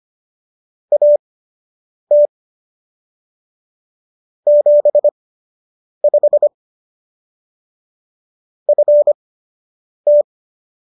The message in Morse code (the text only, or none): AT 75 FT